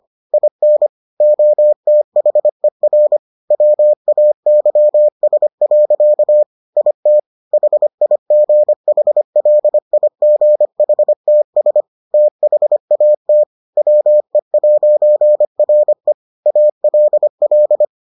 Morse code, heard as a keyed tone, about 25 wpm, 600 Hz. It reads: IN OTHER WAYS. IT HIGHLIGHTS THAT WE'RE ALL